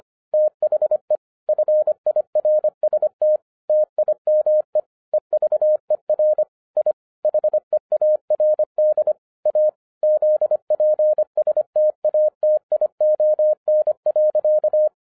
Morse code, heard as a keyed tone, about 25 wpm, 610 Hz. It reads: THE FIRST TIME EVER I HEARD A ZP STATION.
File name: 2012-01-mp3 / n06-FIRST FISTS CONTACT-25wpm